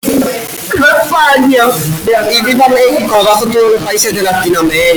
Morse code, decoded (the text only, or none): none